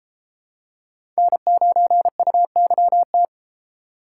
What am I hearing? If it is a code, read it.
Morse code, heard as a keyed tone, about 33 wpm, 700 Hz. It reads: N9UYT